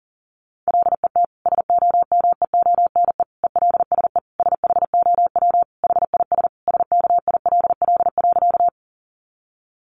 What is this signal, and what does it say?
Morse code, heard as a keyed tone, about 40 wpm, 700 Hz. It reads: LET SOMEONE ELSE SHOW HIS SKILL.
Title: Sentences from Top 1200 Words 40wpm